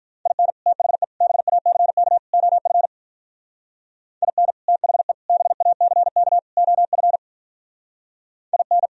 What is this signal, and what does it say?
Morse code, heard as a keyed tone, about 44 wpm, 700 Hz. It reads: IN THE BACK OF IN THE BACK OF IN
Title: Speed-Racing - Sets of 4 Words 35wpm